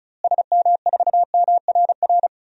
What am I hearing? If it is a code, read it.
Morse code, heard as a keyed tone, about 34 wpm, 700 Hz. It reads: SM4MRR
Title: DXCC-100 Call Signs - 6 Characters - Repeated 2x - 35wpm